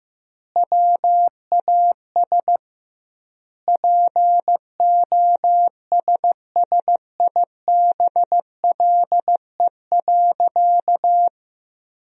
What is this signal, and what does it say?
Morse code, heard as a keyed tone, about 15 wpm, 700 Hz. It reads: WAS POSSIBLE.
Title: 2x Word-Spacing - Sentences from Top 500 Words - 15wpm